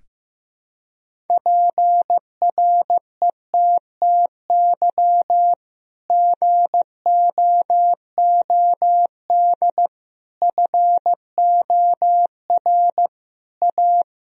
Morse code, heard as a keyed tone, about 16 wpm, 700 Hz. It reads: PRETTY GOOD FOR A
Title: Sentences from Top 1000 Words - Review Part 1 - 15wpm